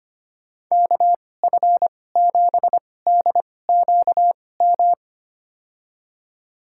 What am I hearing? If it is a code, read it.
Morse code, heard as a keyed tone, about 25 wpm, spaced as for 18 wpm, 700 Hz. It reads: KF7DQM
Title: USA Call Signs - 2x Character Spacing - 25wpm